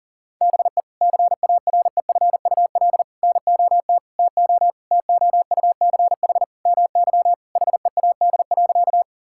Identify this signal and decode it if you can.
Morse code, heard as a keyed tone, about 40 wpm, 700 Hz. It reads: BE CAREFUL NOT TO TOUCH MY HEAD.
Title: Sentences from Top 1600 Words 40wpm